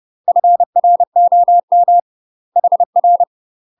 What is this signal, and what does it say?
Morse code, heard as a keyed tone, about 30 wpm, 700 Hz. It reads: FROM HR
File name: UK MASTER QSOs TRAINING / 30WPM / UK MASTER QSOs 0030